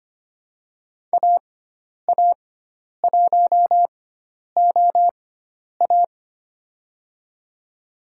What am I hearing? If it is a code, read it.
Morse code, heard as a keyed tone, about 25 wpm, spaced as for 10 wpm, 700 Hz. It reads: AA1OA
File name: Lesson 16g - Letter c - Call Signs 25wpm